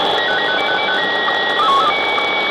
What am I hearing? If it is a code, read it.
SSTV (slow-scan television), Scottie 1.
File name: pirate / 10h22utc-6400khz-jodelpiraat-sstv